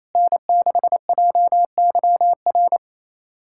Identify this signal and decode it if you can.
Morse code, heard as a keyed tone, about 28 wpm, 700 Hz. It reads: N6JYR
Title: USA Callsign DE Callsign 28wpm